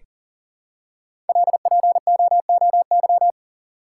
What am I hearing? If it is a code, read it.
Morse code, heard as a keyed tone, about 40 wpm, 700 Hz. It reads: LPOOY
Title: Code Groups - 5 Letters - 40wpm